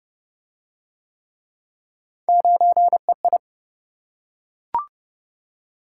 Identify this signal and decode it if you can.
Morse code, heard as a keyed tone, about 28 wpm, 700 Hz. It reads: 9EI